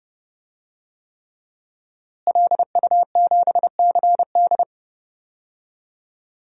Morse code, heard as a keyed tone, about 31 wpm, 700 Hz. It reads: LU7CD
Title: DXCC-100 Call Signs - 5 Characters - Repeated 3x - 30wpm